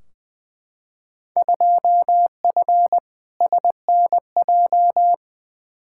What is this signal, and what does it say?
Morse code, heard as a keyed tone, about 20 wpm, 700 Hz. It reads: 2F SNJ
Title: Field Day Exchanges Encore - 20wpm